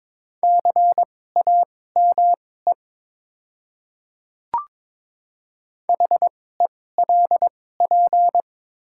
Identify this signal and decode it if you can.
Morse code, heard as a keyed tone, about 22 wpm, spaced as for 16 wpm, 700 Hz. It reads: CAME HELP
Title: 2x Character Spacing - Top 200 Words - 22wpm